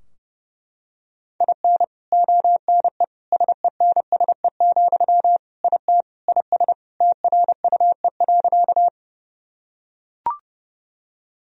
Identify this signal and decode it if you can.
Morse code, heard as a keyed tone, about 28 wpm, 700 Hz. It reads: IN ONE SENSE, IT IS TRUE.
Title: Sentences from Top 1000 Words - Review Part 4 - 30wpm